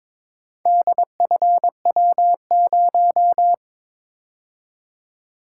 Morse code, heard as a keyed tone, about 22 wpm, 700 Hz. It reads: DFW0
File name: practice / 22wpm / Four Letter-Number Combinations - 22wpm